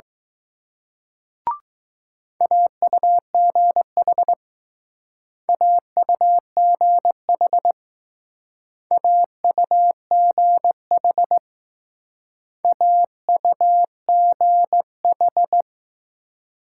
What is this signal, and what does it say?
Morse code, heard as a keyed tone, about 19 wpm, 700 Hz. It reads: AUGH AUGH AUGH AUGH